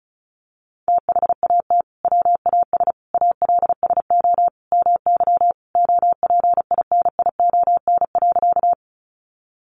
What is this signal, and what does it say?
Morse code, heard as a keyed tone, about 34 wpm, 700 Hz. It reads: THAT WAS ALSO MY OPINION.
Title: Sentences from Top 1200 Words 35wpm